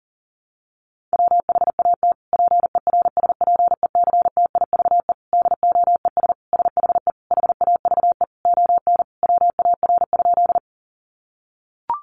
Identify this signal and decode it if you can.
Morse code, heard as a keyed tone, about 40 wpm, 700 Hz. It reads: WHAT PERSPECTIVE DOES SHE HAVE ON WAR?
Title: Sentences from Top 1200 Words 40wpm